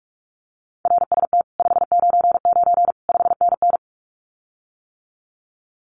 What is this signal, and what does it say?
Morse code, heard as a keyed tone, about 46 wpm, 700 Hz. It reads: RST 599 5NN
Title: Common QSO Phrases Encore 45wpm